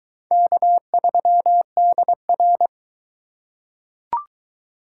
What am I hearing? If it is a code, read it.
Morse code, heard as a keyed tone, about 22 wpm, 700 Hz. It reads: K3DR